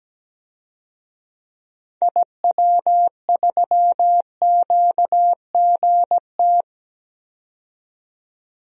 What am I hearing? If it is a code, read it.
Morse code, heard as a keyed tone, about 17 wpm, 700 Hz. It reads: IW3QGT
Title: DXCC-100 Call Signs - 6 Characters - 17wpm